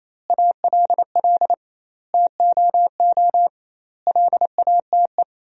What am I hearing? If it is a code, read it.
Morse code, heard as a keyed tone, about 28 wpm, 700 Hz. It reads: ALL TOO LATE
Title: 2x Word-Spacing - Sentences from Top 300 Words - 28wpm